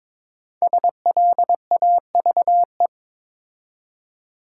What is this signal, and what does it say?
Morse code, heard as a keyed tone, about 22 wpm, 700 Hz. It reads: SLAVE
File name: 5 Letter Words - First 3 Letters Given - 22wpm